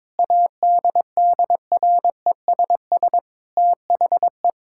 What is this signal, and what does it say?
Morse code, heard as a keyed tone, about 22 wpm, 700 Hz. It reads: ADDRESS THE
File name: Sentences with 7-Word Phrases - 22wpm